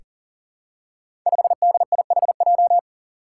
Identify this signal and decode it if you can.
Morse code, heard as a keyed tone, about 40 wpm, 700 Hz. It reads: 5DIHJ
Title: Five Challenging Letter-Number Combinations 40wpm